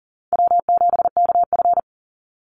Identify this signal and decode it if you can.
Morse code, heard as a keyed tone, about 40 wpm, 700 Hz. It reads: W7KF
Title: Easy Call Signs - 40wpm